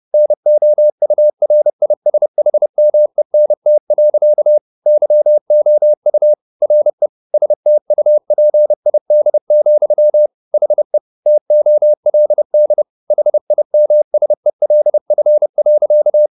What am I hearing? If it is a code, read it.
Morse code, heard as a keyed tone, about 30 wpm, 600 Hz. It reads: NOURISHMENT. YOU RE STUPID, HE TOLD HIMSELF.